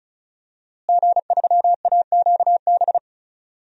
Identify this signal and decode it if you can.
Morse code, heard as a keyed tone, about 36 wpm, 700 Hz. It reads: G3AQB
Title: DXCC-100 Call Signs - 5 Characters - Repeated 2x - 35wpm